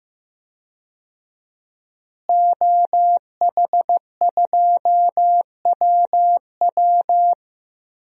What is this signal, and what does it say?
Morse code, heard as a keyed tone, about 15 wpm, 700 Hz. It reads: OH2WW
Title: DXCC-100 Call Signs - 5 Characters - Repeated 3x - 15wpm